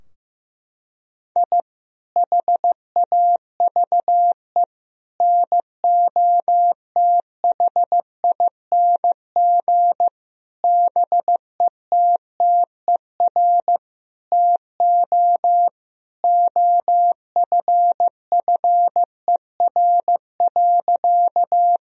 Morse code, heard as a keyed tone, about 15 wpm, 700 Hz. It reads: I HAVE NOTHING BETTER TO OFFER.
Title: Sentences from Top 1000 Words - Review Part 4 - 15wpm